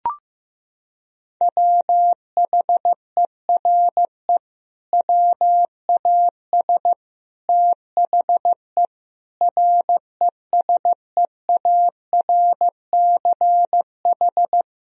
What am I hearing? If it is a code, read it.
Morse code, heard as a keyed tone, about 15 wpm, 700 Hz. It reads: WHERE WAS THE RESEARCH
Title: Sentences from Top 1000 Words - Review Part 4 - 15wpm